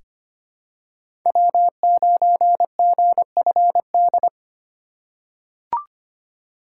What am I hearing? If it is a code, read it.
Morse code, heard as a keyed tone, about 25 wpm, 700 Hz. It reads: W9GFD
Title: USA Call Signs - 4x Character Spacing - 25wpm